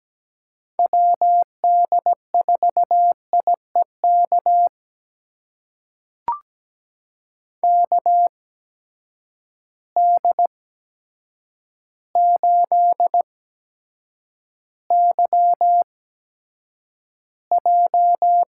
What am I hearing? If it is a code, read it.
Morse code, heard as a keyed tone, about 17 wpm, spaced as for 8 wpm, 700 Hz. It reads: WD4IEK KD8YJ
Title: USA Call Signs - 6 Characters - 8x Character Spacing - 17wpm